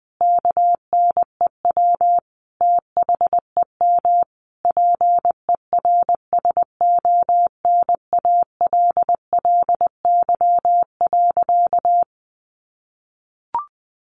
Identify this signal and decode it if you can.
Morse code, heard as a keyed tone, about 20 wpm, 700 Hz. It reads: KNEW THEM PERSONALLY.
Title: Sentences from Top 1400 Words 20wpm